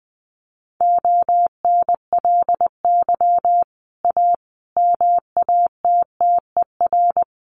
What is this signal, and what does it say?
Morse code, heard as a keyed tone, about 19 wpm, 700 Hz. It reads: ONLY A MATTER
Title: Sets of 5 Words Encore 20wpm